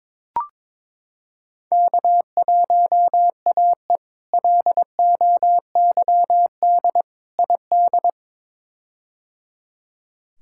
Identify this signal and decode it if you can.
Morse code, heard as a keyed tone, about 22 wpm, 700 Hz. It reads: K1AE LOYD ID
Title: K1USN SST QSO Practice - 22wpm